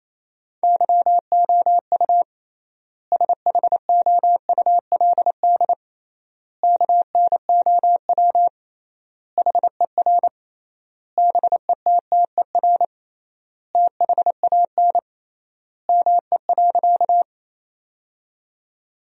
Morse code, heard as a keyed tone, about 28 wpm, 700 Hz. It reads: YOU SHOULD KNOW HER BETTER THAN ME.
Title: 3x Word-Spacing - Sentences from Top 400 Words - 28wpm